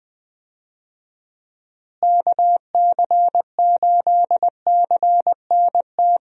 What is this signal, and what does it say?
Morse code, heard as a keyed tone, about 20 wpm, 700 Hz. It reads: KC8CNT
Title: USA Call Signs - 6 Characters - Repeated 2x - 20wpm